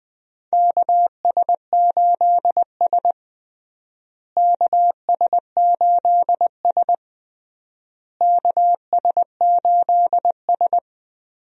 Morse code, heard as a keyed tone, about 20 wpm, 700 Hz. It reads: KS8S KS8S KS8S
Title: USA Call Signs - 4 Characters - Repeated 3x - 20wpm